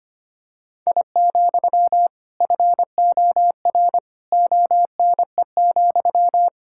Morse code, heard as a keyed tone, about 25 wpm, 700 Hz. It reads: I, FOR ONE,
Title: Sentences from Top 1000 Words - Review Part 5 - 25wpm